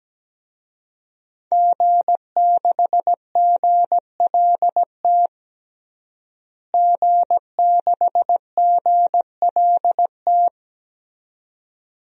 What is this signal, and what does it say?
Morse code, heard as a keyed tone, about 17 wpm, 700 Hz. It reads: G6GLT G6GLT
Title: DXCC-100 Call Signs - 5 Characters - Repeated 2x - 17wpm